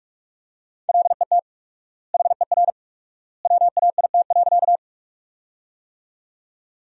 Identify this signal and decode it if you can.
Morse code, heard as a keyed tone, about 44 wpm, 700 Hz. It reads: LET HER WAIT.
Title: 4x Word-Spacing - Sentences from Top 500 Words - 45wpm